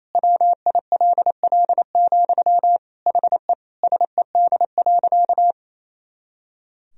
Morse code, heard as a keyed tone, about 28 wpm, 700 Hz. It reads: WILL, HE SED.
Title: Sentences from Top 200 Words and Abbreviations 28wpm